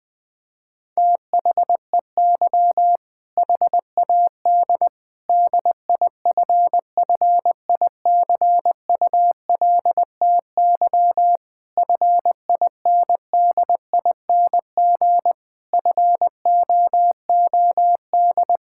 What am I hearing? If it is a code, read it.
Morse code, heard as a keyed tone, about 21 wpm, 700 Hz. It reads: THEY HAD DIFFICULTY FINDING FOOD
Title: Sentences from Top 1600 Words 20wpm